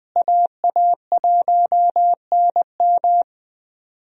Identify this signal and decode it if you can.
Morse code, heard as a keyed tone, about 21 wpm, 700 Hz. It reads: AA1NM
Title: USA Call Signs - 5 Characters - Repeated 3x - 20wpm